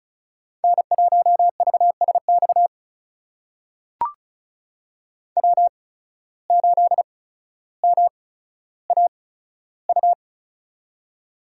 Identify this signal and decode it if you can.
Morse code, heard as a keyed tone, about 36 wpm, spaced as for 15 wpm, 700 Hz. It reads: N1VSX W8MAU
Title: USA Call Signs - 5 Characters - 8x Character Spacing - 35wpm